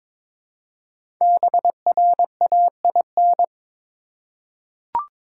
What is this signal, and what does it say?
Morse code, heard as a keyed tone, about 22 wpm, 700 Hz. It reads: BRAIN
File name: 5 Letter Words - First 3 Letters Given - 22wpm